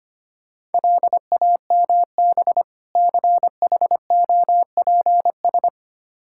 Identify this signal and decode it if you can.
Morse code, heard as a keyed tone, about 25 wpm, 700 Hz. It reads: LAMB CHOPS
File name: practice / 25wpm / Difficult Words with Hints - 25wpm